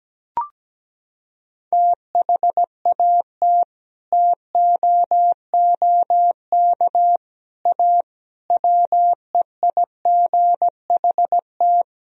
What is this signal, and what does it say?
Morse code, heard as a keyed tone, about 17 wpm, 700 Hz. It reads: THAT TOOK A WEIGHT
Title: Sentences from Top 1000 Words - Review Part 3 - 17wpm